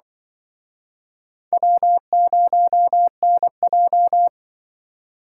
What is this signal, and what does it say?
Morse code, heard as a keyed tone, about 26 wpm, 700 Hz. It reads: W0NJ